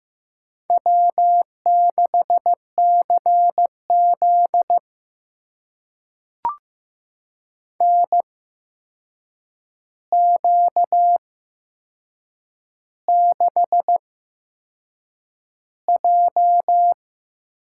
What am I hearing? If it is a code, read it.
Morse code, heard as a keyed tone, about 15 wpm, spaced as for 6 wpm, 700 Hz. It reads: W6CZ NQ6J
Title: USA Call Signs - 4 Characters - 8x Character Spacing - 15wpm